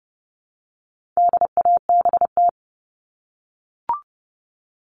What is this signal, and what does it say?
Morse code, heard as a keyed tone, about 29 wpm, 700 Hz. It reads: DABT
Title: All Cut Numbers - 4-Digit 30wpm